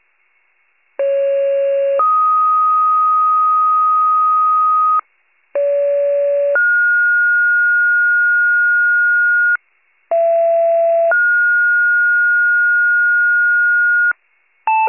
Two-tone page, heard as a one-second tone, then a three-second tone, about 550 then 1250 Hz, about 570 then 1430 Hz, about 670 then 1430 Hz, with noise. It is heard from 1.0 to 14.1 s.